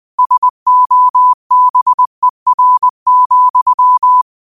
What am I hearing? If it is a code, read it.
Morse code, heard as a keyed tone, about 20 wpm, 1000 Hz. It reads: SOBER,
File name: 20wpm